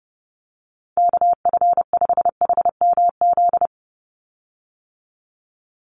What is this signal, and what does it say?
Morse code, heard as a keyed tone, about 30 wpm, 700 Hz. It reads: KF5HMZ
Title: USA Call Signs 30wpm